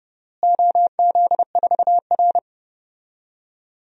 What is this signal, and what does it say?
Morse code, heard as a keyed tone, about 32 wpm, 700 Hz. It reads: OZ4R
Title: DXCC-100 Call Signs - 4 Characters - Repeated 3x - 30wpm